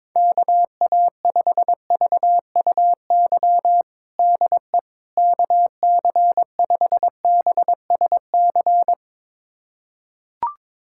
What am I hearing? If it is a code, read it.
Morse code, heard as a keyed tone, about 22 wpm, 700 Hz. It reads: KA5VUY DE KC5BSC
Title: USA Callsign DE Callsign - 6 Characters - 22wpm